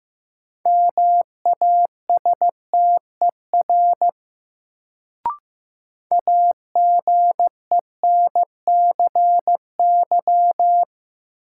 Morse code, heard as a keyed tone, about 15 wpm, 700 Hz. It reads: MASTER AGENCY
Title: Sentences from Top 1300 Words - New Word Drill - 15wpm